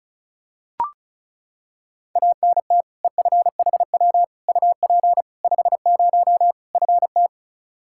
Morse code, heard as a keyed tone, about 36 wpm, 700 Hz. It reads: ANT EFHW UP 50 FT
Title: Practice QSO - Exchange 2 - Info - 35wpm